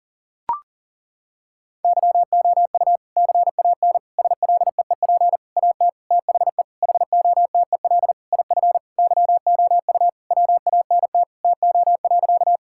Morse code, heard as a keyed tone, about 37 wpm, 700 Hz. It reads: YOU CAN SLEEP AT THE HOTEL IF YOU WANT TO.
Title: Sentences from Top 1600 Words 40wpm